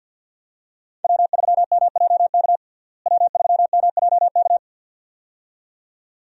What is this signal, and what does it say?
Morse code, heard as a keyed tone, about 52 wpm, 700 Hz. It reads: W3MJK W3MJK
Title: USA Call Signs - 5 Characters - Repeated 2x - 50wpm